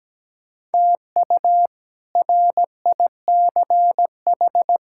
Morse code, heard as a keyed tone, about 17 wpm, 700 Hz. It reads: TU RICH
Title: K1USN SST QSO Practice - 17wpm